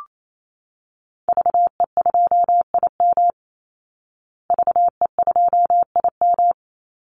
Morse code, heard as a keyed tone, about 28 wpm, 700 Hz. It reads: VE2IM VE2IM
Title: Call Signs Repeated 2x - 28wpm